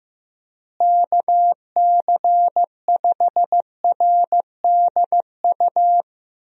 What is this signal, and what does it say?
Morse code, heard as a keyed tone, about 15 wpm, 700 Hz. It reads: KC5RDU
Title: USA Call Signs - 6 Characters - Repeated 3x - 15wpm